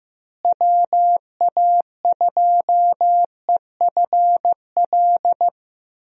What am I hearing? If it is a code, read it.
Morse code, heard as a keyed tone, about 15 wpm, 700 Hz. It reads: WA2EFL
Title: USA Call Signs - 6 Characters - Repeated 3x - 15wpm